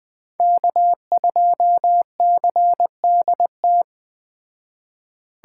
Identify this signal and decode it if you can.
Morse code, heard as a keyed tone, about 20 wpm, 700 Hz. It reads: K2CDT
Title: Speed Racing - USA 1x3 Call Signs 17wpm